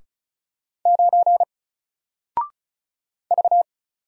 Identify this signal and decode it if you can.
Morse code, heard as a keyed tone, about 36 wpm, 700 Hz. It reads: 9 V
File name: Single-Letter-Number-35wpm